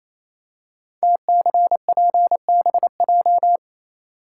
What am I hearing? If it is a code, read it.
Morse code, heard as a keyed tone, about 27 wpm, 700 Hz. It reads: TCPBJ
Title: Code Groups - 5 Letters - 28wpm